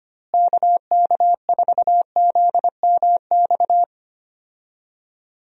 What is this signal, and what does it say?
Morse code, heard as a keyed tone, about 25 wpm, 700 Hz. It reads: KK4ZMX
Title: USA Call Signs - 6 Characters - Repeated 2x - 25wpm